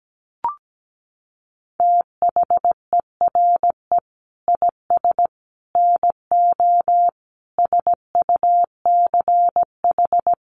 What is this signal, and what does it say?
Morse code, heard as a keyed tone, about 17 wpm, 700 Hz. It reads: THERE IS NO SUCH
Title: Sets of 5 Words Encore 17wpm